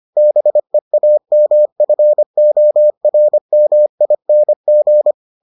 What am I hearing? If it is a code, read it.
Morse code, heard as a keyed tone, about 25 wpm, 600 Hz. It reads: BEAMFORMING